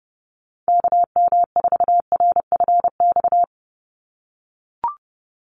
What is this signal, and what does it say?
Morse code, heard as a keyed tone, about 30 wpm, 700 Hz. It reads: KM4RFX
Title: USA Call Signs 30wpm